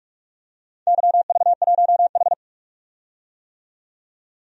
Morse code, heard as a keyed tone, about 44 wpm, 700 Hz. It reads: YV1H
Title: DXCC-100 Call Signs - 4 Characters - 45wpm